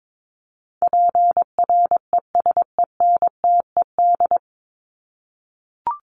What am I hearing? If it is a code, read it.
Morse code, heard as a keyed tone, about 22 wpm, 700 Hz. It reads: PRESENTED
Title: ED Suffix - Top 100 Words with Base Words 22wpm